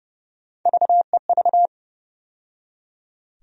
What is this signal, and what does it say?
Morse code, heard as a keyed tone, about 31 wpm, 700 Hz. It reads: VEV
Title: USA Call Sign Postfixes - 3 Letters 30wpm